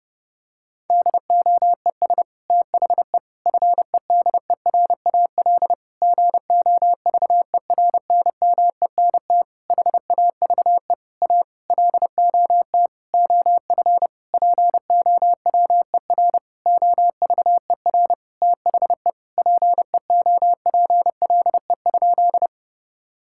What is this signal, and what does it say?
Morse code, heard as a keyed tone, about 30 wpm, 700 Hz. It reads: DOES THE FEDERAL GOVERNMENT HAVE A LOT OF POWER OVER THE PEOPLE?
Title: Sentences from Top 1000 Words - Review Part 5 - 30wpm